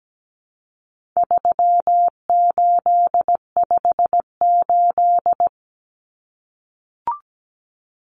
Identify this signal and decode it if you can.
Morse code, heard as a keyed tone, about 17 wpm, 700 Hz. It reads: 3858
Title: Common Cut Numbers - 4-Digit 17wpm